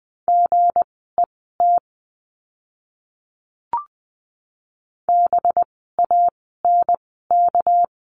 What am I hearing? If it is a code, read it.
Morse code, heard as a keyed tone, about 20 wpm, spaced as for 14 wpm, 700 Hz. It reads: GET BANK
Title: 199 Lesson 32f - Letter k - Words 20wpm